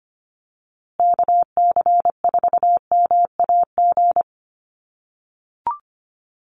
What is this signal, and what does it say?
Morse code, heard as a keyed tone, about 25 wpm, 700 Hz. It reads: KC4MAG
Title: USA Call Signs Repeated 3x 25wpm